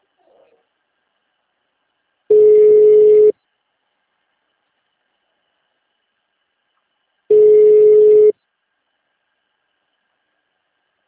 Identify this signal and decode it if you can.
Ringback tone (Europe, 425 Hz).
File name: out- -2032- - - .86660